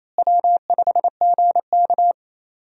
Morse code, heard as a keyed tone, about 28 wpm, 700 Hz. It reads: W5GK
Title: USA Call Signs - 4 Characters - Repeated 3x - 28wpm